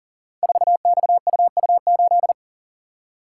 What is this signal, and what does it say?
Morse code, heard as a keyed tone, about 42 wpm, 700 Hz. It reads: 4XUU8
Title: Code Groups - 5 Letter:Numbers - 40wpm